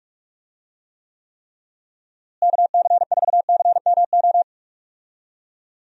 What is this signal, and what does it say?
Morse code, heard as a keyed tone, about 46 wpm, 700 Hz. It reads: KC4CMO
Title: USA Call Signs - 6 Characters - Repeated 3x - 45wpm